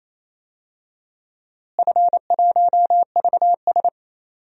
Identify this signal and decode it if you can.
Morse code, heard as a keyed tone, about 27 wpm, 700 Hz. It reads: F1VS
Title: DXCC-100 Call Signs - 4 Characters - Repeated 2x - 28wpm